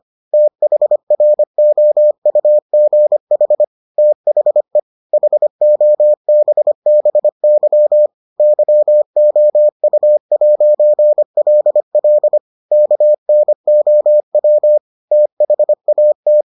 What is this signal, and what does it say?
Morse code, heard as a keyed tone, about 25 wpm, 600 Hz. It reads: THROUGH THE HOBBY YOU'LL KNOW THAT